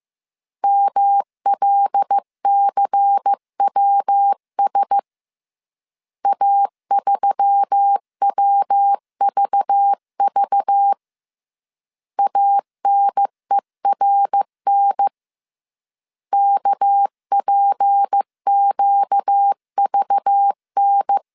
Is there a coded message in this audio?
Yes: Morse code from 0.6 to 21.2 s, heard as a keyed tone, about 15 wpm, 800 Hz. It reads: MLCWS A3WVV ANERN KPQVN